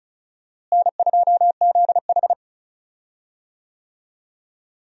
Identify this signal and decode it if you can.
Morse code, heard as a keyed tone, about 36 wpm, 700 Hz. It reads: N2ZH
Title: USA Call Signs - 4 Characters - Repeated 3x - 35wpm